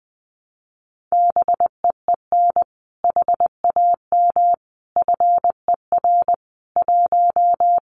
Morse code, heard as a keyed tone, about 20 wpm, 700 Hz. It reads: BEEN HAM FER 1
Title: Common QSO Phrases Encore 20wpm